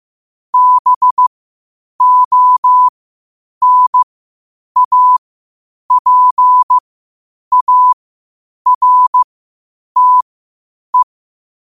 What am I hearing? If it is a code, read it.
Morse code, heard as a keyed tone, about 15 wpm, spaced as for 8 wpm, 1000 Hz. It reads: BONAPARTE